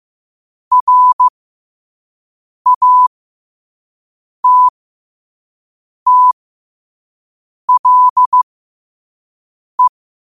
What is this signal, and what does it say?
Morse code, heard as a keyed tone, about 15 wpm, spaced as for 5 wpm, 1000 Hz. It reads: RATTLE
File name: qotdfiles / 5wpm